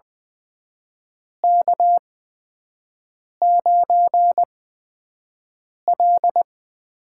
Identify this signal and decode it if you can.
Morse code, heard as a keyed tone, about 20 wpm, spaced as for 5 wpm, 700 Hz. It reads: K9L
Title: USA Call Signs - 4 Characters - 8x Character Spacing - 20wpm